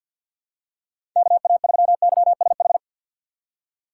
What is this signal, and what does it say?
Morse code, heard as a keyed tone, about 50 wpm, 700 Hz. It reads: KA3YSH
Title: USA Call Signs - 6 Characters - 6x Character Spacing - 50wpm